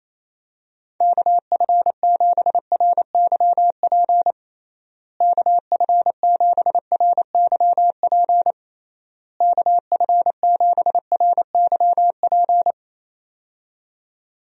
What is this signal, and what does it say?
Morse code, heard as a keyed tone, about 27 wpm, 700 Hz. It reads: KF7RYP KF7RYP KF7RYP
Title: USA Call Signs - 6 Characters - Repeated 3x - 28wpm